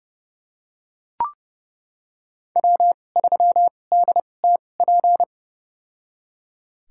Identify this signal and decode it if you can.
Morse code, heard as a keyed tone, about 30 wpm, spaced as for 22 wpm, 700 Hz. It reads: W3DTP
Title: USA Call Signs - 5 Characters - 2x Character Spacing - 30wpm